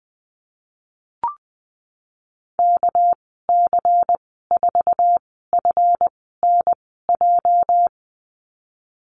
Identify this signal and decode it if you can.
Morse code, heard as a keyed tone, about 20 wpm, spaced as for 15 wpm, 700 Hz. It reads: KC4FNJ
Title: 231 Lesson 37g - Letter x - Call Signs 20wpm